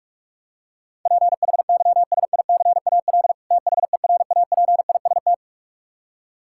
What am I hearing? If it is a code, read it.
Morse code, heard as a keyed tone, about 46 wpm, 700 Hz. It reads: PHYSICAL THERAPIST
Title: Practice QSO - Exchange 3 - More Info - 45wpm